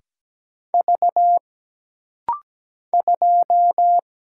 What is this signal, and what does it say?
Morse code, heard as a keyed tone, about 17 wpm, 700 Hz. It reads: V 2
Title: Single Letter-Number - Rapid-Fire 17wpm